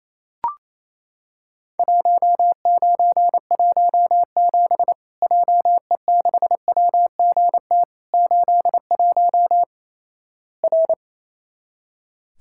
Morse code, heard as a keyed tone, about 28 wpm, 700 Hz. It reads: 1917 JE6WGT 81 R
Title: Worked All Europe DX Contest - QTC Exchange - 28wpm